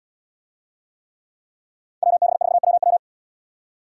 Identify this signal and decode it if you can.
Morse code, heard as a keyed tone, about 77 wpm, 700 Hz. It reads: WB4WW